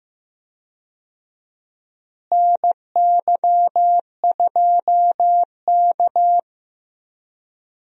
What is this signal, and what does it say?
Morse code, heard as a keyed tone, about 15 wpm, 700 Hz. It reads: NY2K
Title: USA Call Signs - 4 Characters - Repeated 3x - 15wpm